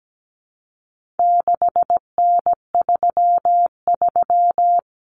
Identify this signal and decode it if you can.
Morse code, heard as a keyed tone, about 17 wpm, 700 Hz. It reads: 6N33
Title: Common Cut Numbers - 4-Digit 17wpm